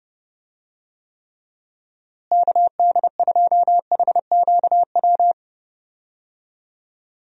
Morse code, heard as a keyed tone, about 30 wpm, 700 Hz. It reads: KD2HQW
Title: USA Call Signs - 6 Characters - Repeated 2x - 30wpm